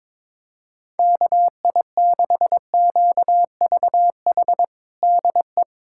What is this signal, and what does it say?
Morse code, heard as a keyed tone, about 21 wpm, 700 Hz. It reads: KI6QVH DE
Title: USA Callsign DE Callsign 22wpm